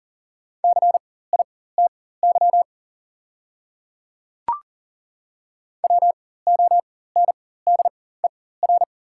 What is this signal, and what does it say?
Morse code, heard as a keyed tone, about 40 wpm, spaced as for 19 wpm, 690 Hz. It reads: CITY WONDER
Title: 4x Character Spacing - Top 500 Words - 40wpm